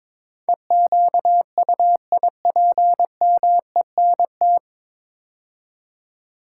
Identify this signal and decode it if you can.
Morse code, heard as a keyed tone, about 22 wpm, 700 Hz. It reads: EQUIPMENT
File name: practice / 22wpm / 9 Letter Words - First 3 Letters Given - 22wpm